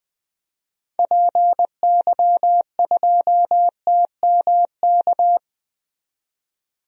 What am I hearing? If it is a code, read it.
Morse code, heard as a keyed tone, about 21 wpm, 700 Hz. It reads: PY2TMK
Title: DXCC-100 Call Signs - 6 Characters - 20wpm